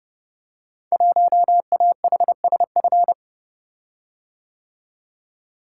Morse code, heard as a keyed tone, about 28 wpm, 700 Hz. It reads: 1AHSF
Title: Code Groups - 5 Letter:Numbers - 30wpm